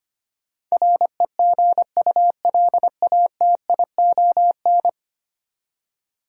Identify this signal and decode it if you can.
Morse code, heard as a keyed tone, about 25 wpm, 700 Hz. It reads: REGULATION
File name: Ten Letter Words - 25wpm